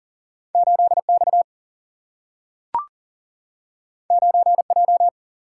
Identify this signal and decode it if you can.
Morse code, heard as a keyed tone, about 40 wpm, 700 Hz. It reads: 8 X 9 J
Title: Single-Letter Single-Number 40wpm